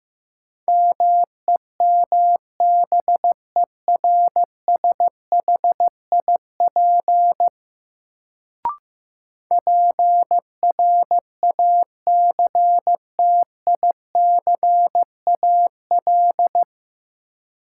Morse code, heard as a keyed tone, about 15 wpm, 700 Hz. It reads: MEMBERSHIP PRACTICAL
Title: Sentences from Top 1400 Words - New Word Drill - 15wpm